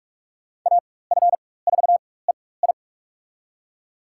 Morse code, heard as a keyed tone, about 41 wpm, spaced as for 20 wpm, 700 Hz. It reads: AF4EI
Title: USA Call Signs - 5 Characters - 4x Character Spacing - 45wpm